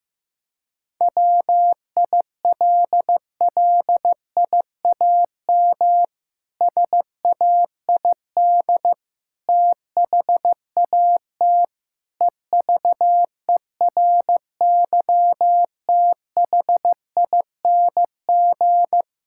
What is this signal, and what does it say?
Morse code, heard as a keyed tone, about 15 wpm, 700 Hz. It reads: WILLIAM SAID THAT EVERYTHING
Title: Pangram Challenge v2 - 15wpm